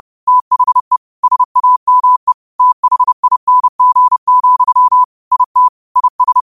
Morse code, heard as a keyed tone, about 30 wpm, 1000 Hz. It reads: THE SAME THING, IT IS